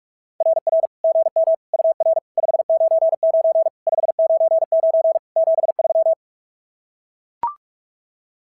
Morse code, heard as a keyed tone, about 42 wpm, 660 Hz. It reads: RR GM UR 599 599 73
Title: SOTA QSO Practice 45wpm